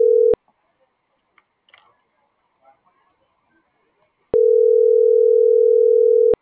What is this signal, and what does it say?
Ringback tone (North America), 440 + 480 Hz.